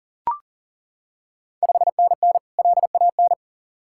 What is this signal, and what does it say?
Morse code, heard as a keyed tone, about 39 wpm, 690 Hz. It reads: 5NN LAN
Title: Ontario QSO Party Exchange 40wpm